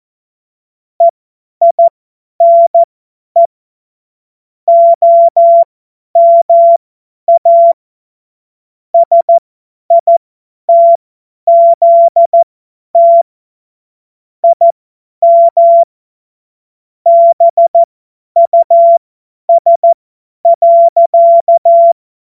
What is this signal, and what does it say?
Morse code, heard as a keyed tone, about 14 wpm, spaced as for 10 wpm, 680 Hz. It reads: EINE OMA SITZT IM BUS.
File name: Morsekurs / Witze / Witz 19. Oktober 2021, 50 F70